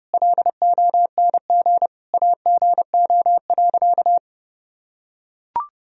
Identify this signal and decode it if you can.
Morse code, heard as a keyed tone, about 32 wpm, 700 Hz. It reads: LONG AGO.
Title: Sentences from Top 1000 Words - Review Part 3 - 30wpm